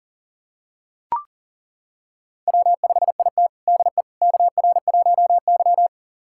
Morse code, heard as a keyed tone, about 40 wpm, 700 Hz. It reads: W5IT DE KR1Y
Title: USA Callsign DE Callsign - 4 Characters - 40wpm